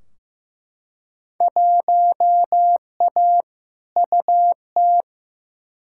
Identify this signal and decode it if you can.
Morse code, heard as a keyed tone, about 15 wpm, 700 Hz. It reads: 1A UT
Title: Field Day Exchanges - 15wpm